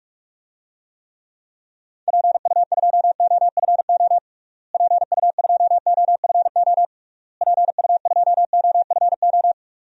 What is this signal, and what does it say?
Morse code, heard as a keyed tone, about 42 wpm, 700 Hz. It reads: PU2OFO PU2OFO PU2OFO
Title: DXCC-100 Call Signs - 6 Characters - Repeated 3x - 45wpm